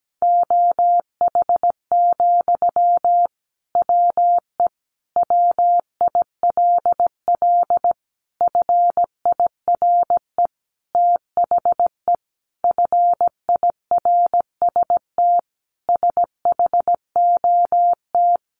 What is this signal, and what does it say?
Morse code, heard as a keyed tone, about 17 wpm, 700 Hz. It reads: OH, WE WILL FIRE THE FIRST SHOT
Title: Sentences from Top 1100 Words 17wpm